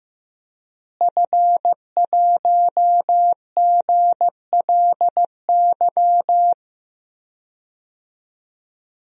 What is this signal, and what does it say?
Morse code, heard as a keyed tone, about 15 wpm, 700 Hz. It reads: F1GLY
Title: DXCC-100 Call Signs - 5 Characters - Repeated 2x - 15wpm